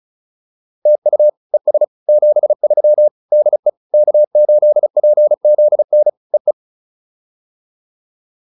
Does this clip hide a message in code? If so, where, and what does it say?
Yes: Morse code from 0.9 to 6.5 s, heard as a keyed tone, about 34 wpm, 600 Hz. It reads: TU ES 73 DE K8PZN EE